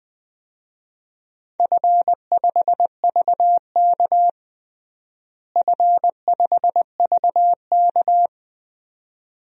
Morse code, heard as a keyed tone, about 21 wpm, 700 Hz. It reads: F5VK F5VK
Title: DXCC-100 Call Signs - 4 Characters - Repeated 2x - 20wpm